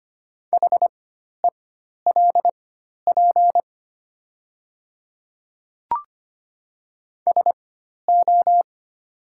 Morse code, heard as a keyed tone, about 25 wpm, spaced as for 12 wpm, 700 Hz. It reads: HELP SO
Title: 4x Character Spacing - Top 200 Words - 25wpm